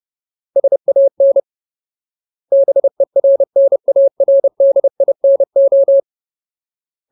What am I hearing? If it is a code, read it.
Morse code, heard as a keyed tone, about 30 wpm, 550 Hz. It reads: SAN BERNARDINO
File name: Ultimate Word List / 30WPM Ultimate